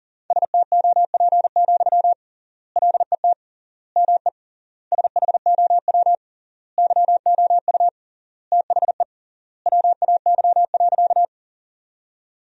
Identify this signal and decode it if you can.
Morse code, heard as a keyed tone, about 37 wpm, 700 Hz. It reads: STOP, LET ME SHOW YOU THE WAY.
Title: 3x Word-Spacing - Sentences from Top 300 Words - 40wpm